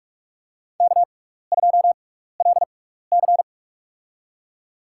Morse code, heard as a keyed tone, about 46 wpm, spaced as for 15 wpm, 700 Hz. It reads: K2LC
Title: USA Call Signs - 4 Characters - 6x Character Spacing - 45wpm